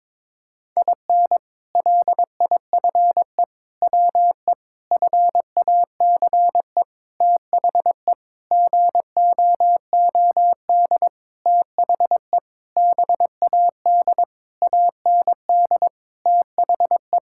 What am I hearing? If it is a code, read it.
Morse code, heard as a keyed tone, about 22 wpm, 700 Hz. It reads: IN LIFE WE FACE THE GOOD THE BAD AND THE
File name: Sentences with 7-Word Phrases - 22wpm